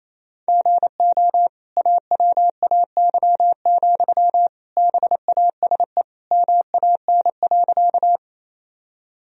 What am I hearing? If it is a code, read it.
Morse code, heard as a keyed tone, about 28 wpm, 700 Hz. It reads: GO AWAY, BASE MAN.
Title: Sentences from Top 1000 Words - Review Part 1 - 28wpm